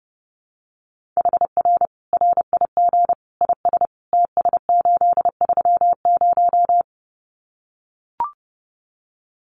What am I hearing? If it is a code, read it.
Morse code, heard as a keyed tone, about 30 wpm, 700 Hz. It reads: HR RIG IS TS830
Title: Common QSO Phrases 30wpm